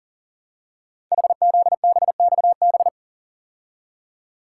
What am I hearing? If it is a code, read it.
Morse code, heard as a keyed tone, about 41 wpm, 700 Hz. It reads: HZBXB
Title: Code Groups - 5 Letters - 40wpm